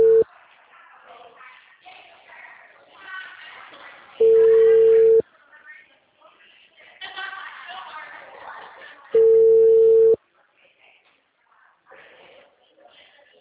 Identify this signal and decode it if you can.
Ringback tone (China, 450 Hz), with speech.
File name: out- -2025